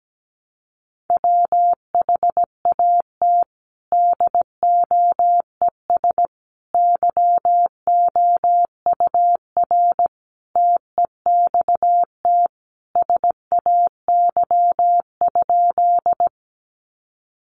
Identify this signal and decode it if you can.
Morse code, heard as a keyed tone, about 17 wpm, 700 Hz. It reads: WHAT DOES YOUR TEXT SAY?
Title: Sentences from Top 1200 Words 17wpm